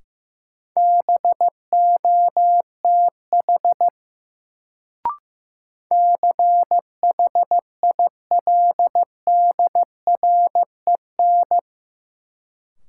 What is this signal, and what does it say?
Morse code, heard as a keyed tone, about 15 wpm, 700 Hz. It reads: BOTH CHILDREN
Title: Sentences from Top 300 Words - New Word Drill - 15wpm